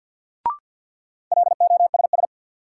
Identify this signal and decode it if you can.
Morse code, heard as a keyed tone, about 52 wpm, 700 Hz. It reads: LOSS
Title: Sentences from Top 1400 Words - New Word Drill - 50wpm